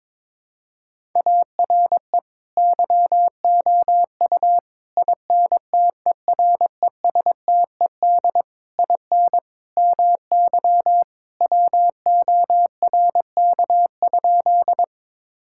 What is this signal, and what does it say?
Morse code, heard as a keyed tone, about 22 wpm, 700 Hz. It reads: ARE YOU INTERESTED IN MY WORK?
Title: Sentences from Top 1000 Words - Review Part 6 - 22wpm